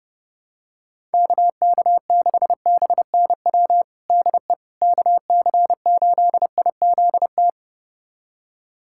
Morse code, heard as a keyed tone, about 30 wpm, 700 Hz. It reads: KK6BNW DE KC8IZT
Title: USA Callsign DE Callsign 30wpm